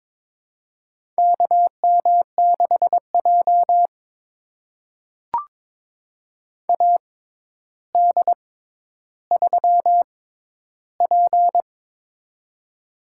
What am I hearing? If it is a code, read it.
Morse code, heard as a keyed tone, about 22 wpm, spaced as for 11 wpm, 700 Hz. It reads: KM6J AD3P
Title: USA Call Signs - 4 Characters - 6x Character Spacing - 22wpm